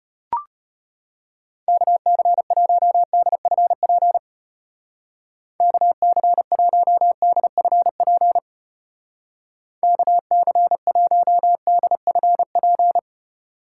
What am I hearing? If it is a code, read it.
Morse code, heard as a keyed tone, about 31 wpm, 700 Hz. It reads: KC1DFP KC1DFP KC1DFP
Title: Speed Racing - USA 2x3 Call Signs 25wpm